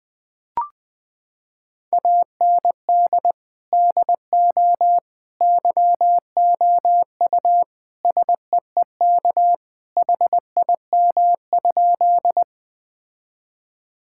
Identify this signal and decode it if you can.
Morse code, heard as a keyed tone, about 20 wpm, 700 Hz. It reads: AND DO YOU SEEK HIM?
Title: Sentences from Top 1600 Words 20wpm